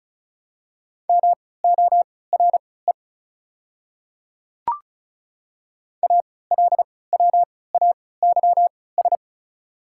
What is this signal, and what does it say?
Morse code, heard as a keyed tone, about 34 wpm, spaced as for 20 wpm, 690 Hz. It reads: MORE ALWAYS
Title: 3x Character Spacing - Top 500 Words - 35wpm